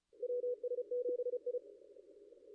Morse code, heard as a keyed tone, about 33 wpm, 490 Hz. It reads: WS7I
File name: contests / ss-cw-k5tr-op-n5rz -2200